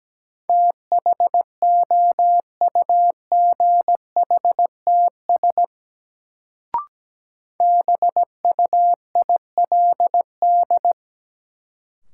Morse code, heard as a keyed tone, about 17 wpm, 700 Hz. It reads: THOUGHTS BUILD
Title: Sentences from Top 1000 Words - Review Part 1 - New Word Drill - 17wpm